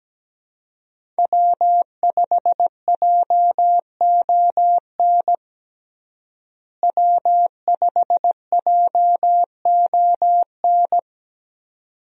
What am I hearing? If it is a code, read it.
Morse code, heard as a keyed tone, about 17 wpm, 700 Hz. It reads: W5JON W5JON
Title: USA Call Signs - 5 Characters - Repeated 2x - 17wpm